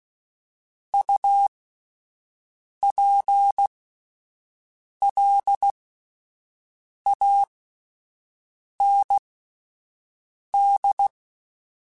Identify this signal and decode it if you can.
Morse code, heard as a keyed tone, about 16 wpm, spaced as for 5 wpm, 800 Hz. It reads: UPLAND